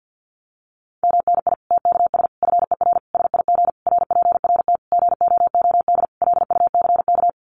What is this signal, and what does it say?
Morse code, heard as a keyed tone, about 50 wpm, 700 Hz. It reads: MNI TKS FER SIG RPRT GOOD LUCK
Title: Common QSO Phrases 50wpm